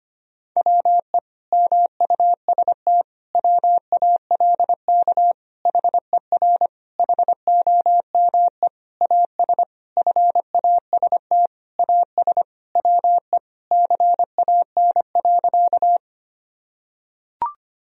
Morse code, heard as a keyed tone, about 25 wpm, 700 Hz. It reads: WE MUST WALK HER HOME AS FAST AS WE CAN.
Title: Sentences from Top 1000 Words - Review Part 2 - 25wpm